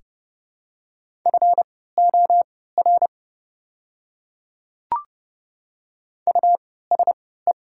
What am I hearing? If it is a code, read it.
Morse code, heard as a keyed tone, about 30 wpm, spaced as for 17 wpm, 700 Hz. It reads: FOR USE
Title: 3x Character Spacing - Top 100 Words - 30wpm